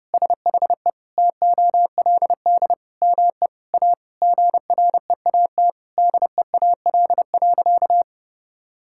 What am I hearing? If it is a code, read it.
Morse code, heard as a keyed tone, about 30 wpm, 700 Hz. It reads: SHE TOLD ME A GREAT DEAL.
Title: Sentences from Top 1000 Words - Review Part 6 - 30wpm